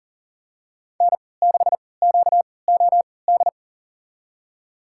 Morse code, heard as a keyed tone, about 37 wpm, spaced as for 22 wpm, 700 Hz. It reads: N6QOD
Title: USA Call Signs - 5 Characters - 3x Character Spacing - 40wpm